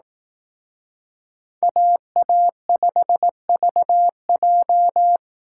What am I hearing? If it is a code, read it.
Morse code, heard as a keyed tone, about 18 wpm, 700 Hz. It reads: AA5VJ
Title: Speed Racing - USA 2x2 Call Signs 15wpm